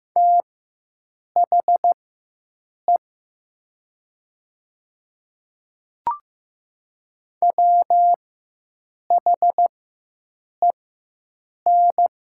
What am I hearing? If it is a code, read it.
Morse code, heard as a keyed tone, about 15 wpm, spaced as for 7 wpm, 700 Hz. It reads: THE WHEN
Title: 4x Character Spacing - Top 100 Words - 15wpm